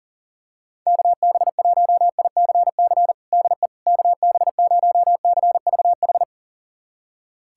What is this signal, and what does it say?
Morse code, heard as a keyed tone, about 40 wpm, 700 Hz. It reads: KB1ICC DE KB0CVH
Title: USA Callsign DE Callsign 40wpm